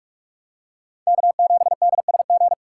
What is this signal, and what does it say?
Morse code, heard as a keyed tone, about 46 wpm, 700 Hz. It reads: K7DSG
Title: USA Call Signs - 5 Characters - 2x Character Spacing - 45wpm